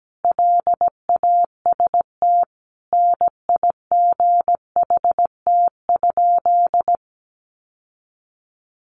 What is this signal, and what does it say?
Morse code, heard as a keyed tone, about 17 wpm, 700 Hz. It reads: LAST NIGHT?
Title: Sentences from Top 600 Words 17wpm